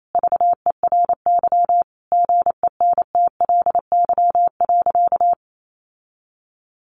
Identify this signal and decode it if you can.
Morse code, heard as a keyed tone, about 28 wpm, 700 Hz. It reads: VERY GENTLY.
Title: Sentences from Top 1500 Words 28wpm